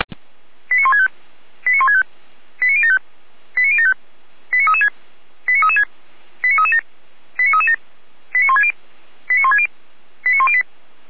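Five-tone selective call (ZVEI), heard as quick runs of five tones, tones of about 70 ms. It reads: 89165 89165 89075 89075 89336 89336 89337 89337 89160 89160 89108
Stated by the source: Five-tone ZVEI selective call received on an SDR, recorded by Feuerwehr Orlamünde.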